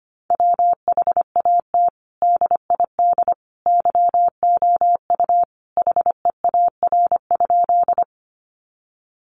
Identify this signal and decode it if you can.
Morse code, heard as a keyed tone, about 25 wpm, 700 Hz. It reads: WHAT DID YOU HEAR?
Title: Sentences from Top 300 Words 25wpm